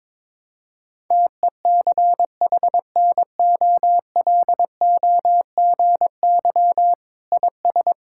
Morse code, heard as a keyed tone, about 22 wpm, 700 Hz. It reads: TECHNOLOGY IS
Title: Sentences from Top 1000 Words - Review Part 5 - 22wpm